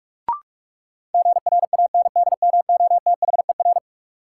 Morse code, heard as a keyed tone, about 46 wpm, 700 Hz. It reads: GRANDMOTHER
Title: Sentences from Top 1200 Words - New Word Drill - 45wpm